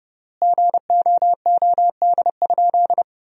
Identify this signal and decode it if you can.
Morse code, heard as a keyed tone, about 30 wpm, 700 Hz. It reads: GOOD?
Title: 3x Word-Spacing - Sentences from Top 200 Words - 30wpm